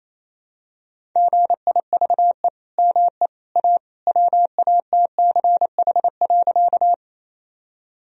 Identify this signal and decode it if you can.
Morse code, heard as a keyed tone, about 28 wpm, 700 Hz. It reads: GIVE ME A WATCH.
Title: Sentences from Top 1000 Words - Review Part 1 - 28wpm